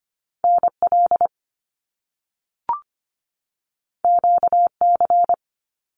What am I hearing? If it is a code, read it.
Morse code, heard as a keyed tone, about 25 wpm, 700 Hz. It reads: NL QC
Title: Canadian Provinces and Territories 25wpm